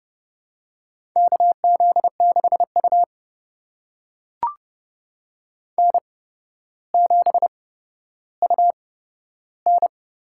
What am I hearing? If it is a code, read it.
Morse code, heard as a keyed tone, about 31 wpm, spaced as for 13 wpm, 700 Hz. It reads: KZ6U N7UN
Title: USA Call Signs - 4 Characters - 8x Character Spacing - 30wpm